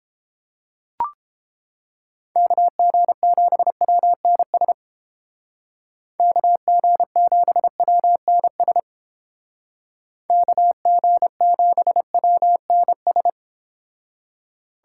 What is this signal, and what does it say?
Morse code, heard as a keyed tone, about 29 wpm, 700 Hz. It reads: KG7WNS KG7WNS KG7WNS